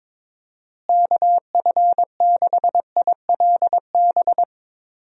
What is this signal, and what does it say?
Morse code, heard as a keyed tone, about 22 wpm, 700 Hz. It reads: KF6ILB
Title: USA Call Signs - 6 Characters - Repeated 2x - 22wpm